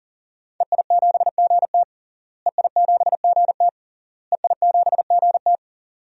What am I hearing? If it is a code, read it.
Morse code, heard as a keyed tone, about 40 wpm, 700 Hz. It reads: EI7GT EI7GT EI7GT
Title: DXCC-100 Call Signs - 5 Characters - Repeated 3x - 40wpm